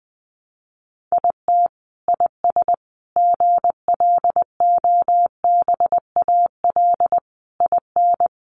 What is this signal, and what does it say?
Morse code, heard as a keyed tone, about 20 wpm, 700 Hz. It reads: IT IS GLOBAL IN
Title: Sentences from Top 1200 Words 20wpm